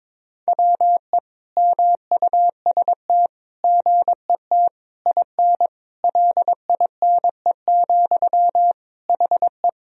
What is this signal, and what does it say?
Morse code, heard as a keyed tone, about 22 wpm, 700 Hz. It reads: WE MUST GET IN LINE, HE
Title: Sentences from Top 1000 Words - Review Part 1 - 22wpm